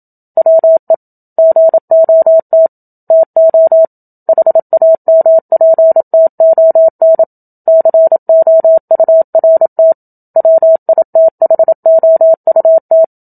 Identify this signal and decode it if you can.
Morse code, heard as a keyed tone, about 27 wpm, 650 Hz. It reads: WE GOT TO HAMPTON COURT WITHOUT
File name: Book Two / 01-Chapter 1 - 27 WPM